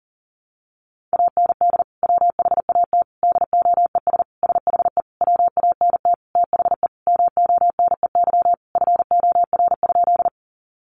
Morse code, heard as a keyed tone, about 40 wpm, 700 Hz. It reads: AND WHAT DOES SHE WANT THE MONEY FOR?
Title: Sentences from Top 400 Words 40wpm